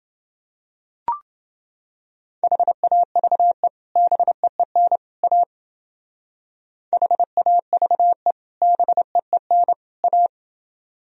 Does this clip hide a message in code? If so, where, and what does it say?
Yes: Morse code from 2.4 to 10.3 s, heard as a keyed tone, about 29 wpm, 700 Hz. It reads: HAVE BEEN A HAVE BEEN A